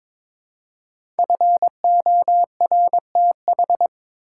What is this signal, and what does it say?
Morse code, heard as a keyed tone, about 23 wpm, 700 Hz. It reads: FORTH
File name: 5 Letter Words - First 3 Letters Given - 22wpm